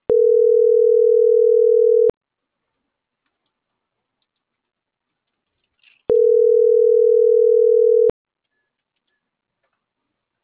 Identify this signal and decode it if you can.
Ringback tone (North America), 440 + 480 Hz.